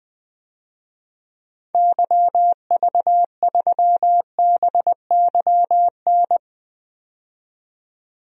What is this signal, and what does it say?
Morse code, heard as a keyed tone, about 20 wpm, 700 Hz. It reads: YV3BYN
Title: DXCC-100 Call Signs - 6 Characters - 20wpm